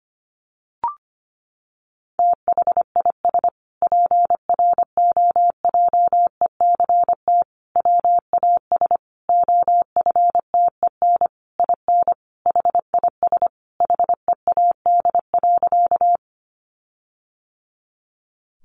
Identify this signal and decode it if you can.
Morse code, heard as a keyed tone, about 25 wpm, 700 Hz. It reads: THIS PROJECT WAS OFTEN IN HIS HEAD.
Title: Sentences from Top 1400 Words 25wpm